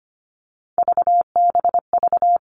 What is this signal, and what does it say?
Morse code, heard as a keyed tone, about 25 wpm, 700 Hz. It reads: VBV
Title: All Cut Numbers - 3-Digit 25wpm